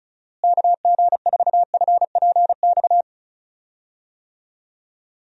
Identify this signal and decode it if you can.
Morse code, heard as a keyed tone, about 36 wpm, 700 Hz. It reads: KG4FPX
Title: USA Call Signs - 6 Characters - Repeated 3x - 35wpm